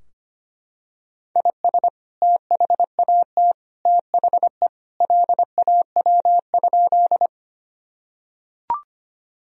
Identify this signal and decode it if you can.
Morse code, heard as a keyed tone, about 25 wpm, 700 Hz. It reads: IS THAT THE LAW?
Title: Sentences from Top 1000 Words - Review Part 3 - 25wpm